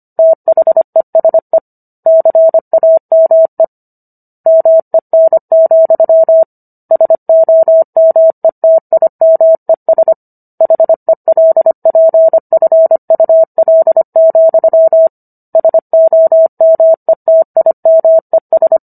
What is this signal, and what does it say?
Morse code, heard as a keyed tone, about 25 wpm, 650 Hz. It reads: THESE CAME MEN, SOMETIMES HELPFUL, SOMETIMES